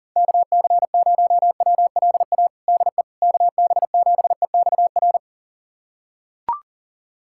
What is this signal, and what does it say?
Morse code, heard as a keyed tone, about 40 wpm, 700 Hz. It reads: KC0WLA DE KB7EXR
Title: USA Callsign DE Callsign 40wpm